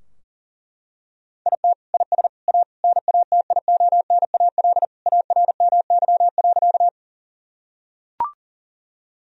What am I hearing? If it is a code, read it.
Morse code, heard as a keyed tone, about 40 wpm, 700 Hz. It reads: IT IS A NATIONAL ARMY.
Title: Sentences from Top 1000 Words - Review Part 4 - 40wpm